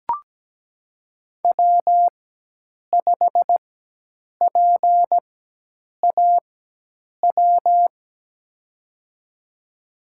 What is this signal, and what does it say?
Morse code, heard as a keyed tone, about 17 wpm, spaced as for 8 wpm, 700 Hz. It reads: W5PAW
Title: USA Call Signs - 5 Characters - 4x Character Spacing - 17wpm